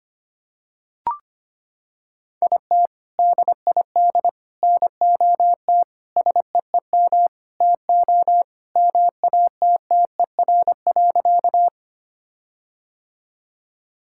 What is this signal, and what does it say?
Morse code, heard as a keyed tone, about 25 wpm, 700 Hz. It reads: IT DID NOT SEEM TO MATTER.
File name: practice / 25wpm / Sentences with 4-Word Phrases - 25wpm